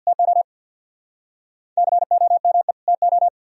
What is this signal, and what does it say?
Morse code, heard as a keyed tone, about 59 wpm, 700 Hz. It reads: TO COME TO
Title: Speed-Racing - Sets of 2 Words 50wpm